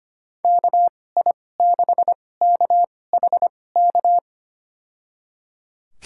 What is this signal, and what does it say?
Morse code, heard as a keyed tone, about 25 wpm, spaced as for 18 wpm, 700 Hz. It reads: KI6KHK
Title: USA Call Signs - 6 Characters - 2x Character Spacing - 25wpm